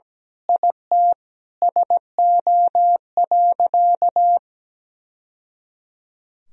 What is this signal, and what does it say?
Morse code, heard as a keyed tone, about 17 wpm, 700 Hz. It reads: IT SO.
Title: Sentences from Top 1000 Words - Review Part 4 - 17wpm